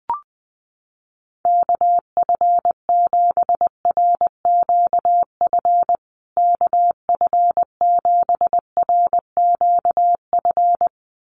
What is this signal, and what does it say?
Morse code, heard as a keyed tone, about 20 wpm, 700 Hz. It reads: KF7RQF KF7RQF
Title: USA Call Signs Repeated 2x 20wpm